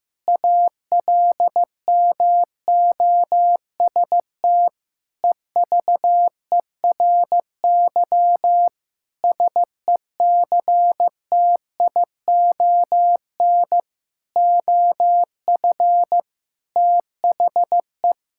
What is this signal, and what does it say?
Morse code, heard as a keyed tone, about 15 wpm, 700 Hz. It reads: ALMOST EVERY SECTION OF THE
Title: Sentences from Top 1000 Words - Review Part 5 - 15wpm